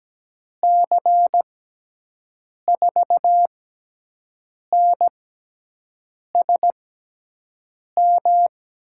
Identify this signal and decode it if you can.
Morse code, heard as a keyed tone, about 17 wpm, spaced as for 6 wpm, 700 Hz. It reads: C4NSM
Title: USA Call Signs - 6 Characters - 6x Character Spacing - 17wpm